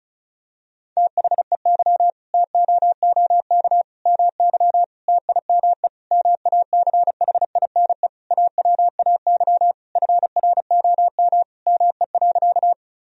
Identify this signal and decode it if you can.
Morse code, heard as a keyed tone, about 36 wpm, 700 Hz. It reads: THEY TOOK MY TIME MACHINE AWAY FROM ME.
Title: Sentences from Top 1000 Words - Review Part 2 - 35wpm